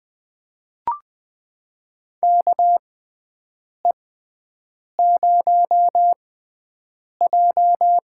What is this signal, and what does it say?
Morse code, heard as a keyed tone, about 20 wpm, spaced as for 7 wpm, 700 Hz. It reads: KE0J
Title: USA Call Signs - 6 Characters - 6x Character Spacing - 20wpm